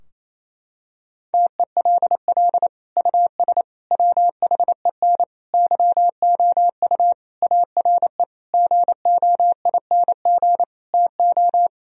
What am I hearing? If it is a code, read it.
Morse code, heard as a keyed tone, about 27 wpm, 700 Hz. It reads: TELL US WHEN YOU ARE GOING TO
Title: Sentences from Top 1000 Words - Review Part 2 - 28wpm